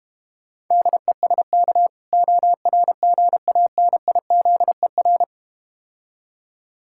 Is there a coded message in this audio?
Yes: Morse code from 0.7 to 5.2 s, heard as a keyed tone, about 32 wpm, 700 Hz. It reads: DESK ORGANIZER